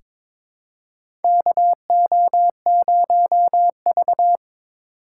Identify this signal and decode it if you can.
Morse code, heard as a keyed tone, about 22 wpm, 700 Hz. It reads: KO0V
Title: USA Call Signs - 4 Characters - 6x Character Spacing - 22wpm